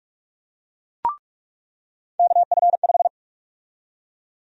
Morse code, heard as a keyed tone, about 46 wpm, 700 Hz. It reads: KF5
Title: USA Call Sign Prefixes - 45wpm